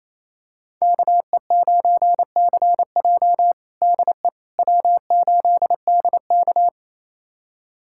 Morse code, heard as a keyed tone, about 28 wpm, 700 Hz. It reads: KE9CJ DE W8DK
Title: USA Callsign DE Callsign 28wpm